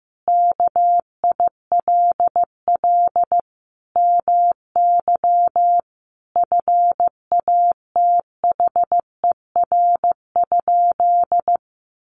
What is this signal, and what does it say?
Morse code, heard as a keyed tone, about 15 wpm, 700 Hz. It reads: KILL MY FATHER?
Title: Sentences from Top 1400 Words 15wpm